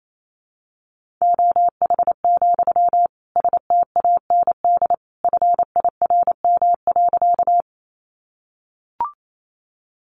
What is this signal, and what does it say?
Morse code, heard as a keyed tone, about 28 wpm, 700 Hz. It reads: OH, STAND FIRM.
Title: Sentences from Top 1400 Words 28wpm